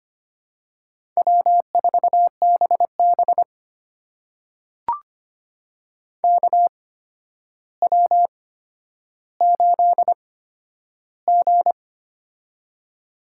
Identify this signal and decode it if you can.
Morse code, heard as a keyed tone, about 25 wpm, spaced as for 11 wpm, 700 Hz. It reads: W4BB KW8G
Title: USA Call Signs - 4 Characters - 8x Character Spacing - 25wpm